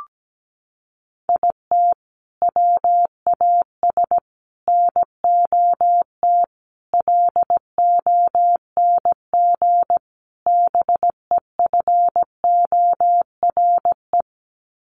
Morse code, heard as a keyed tone, about 17 wpm, 700 Hz. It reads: IT WAS NOT LONG BEFORE
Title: Sets of 5 Words Encore2 17wpm